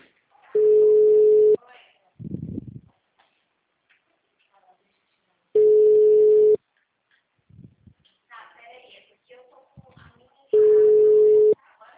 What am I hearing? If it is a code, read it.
Ringback tone (Europe, 425 Hz), with speech.